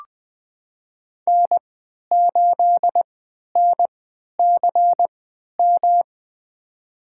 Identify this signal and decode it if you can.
Morse code, heard as a keyed tone, about 20 wpm, spaced as for 11 wpm, 700 Hz. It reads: N8NCM
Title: USA Call Signs - 5 Characters - 3x Character Spacing - 20wpm